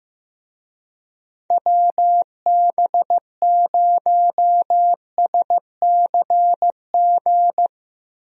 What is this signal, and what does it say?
Morse code, heard as a keyed tone, about 16 wpm, 700 Hz. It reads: WB0SCG
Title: USA Call Signs - 6 Characters - Repeated 2x - 15wpm